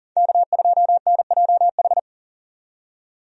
Morse code, heard as a keyed tone, about 40 wpm, 700 Hz. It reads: K2NJH
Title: USA Call Signs - 5 Characters - 40wpm